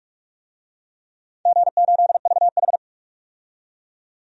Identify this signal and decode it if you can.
Morse code, heard as a keyed tone, about 42 wpm, 700 Hz. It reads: G8VH
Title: DXCC-100 Call Signs - 4 Characters - 45wpm